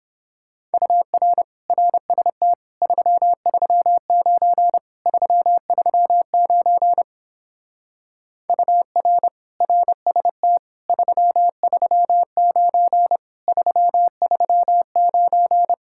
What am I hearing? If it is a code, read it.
Morse code, heard as a keyed tone, about 29 wpm, 700 Hz. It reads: UR RST 339 339 UR RST 339 339